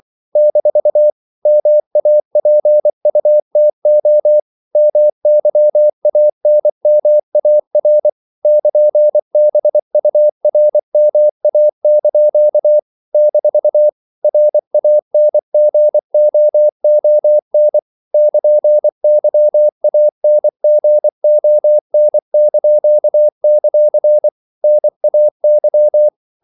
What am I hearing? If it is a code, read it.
Morse code, heard as a keyed tone, about 24 wpm, 600 Hz. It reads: - MAPUTO MYANMAR <KN>BURMA) - RANGOON <KN>YANGON); NAY
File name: CW / Practice / NationalCapitals24